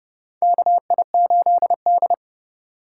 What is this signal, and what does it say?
Morse code, heard as a keyed tone, about 30 wpm, 700 Hz. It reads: KI8D
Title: USA Call Signs - 4 Characters - 4x Character Spacing - 30wpm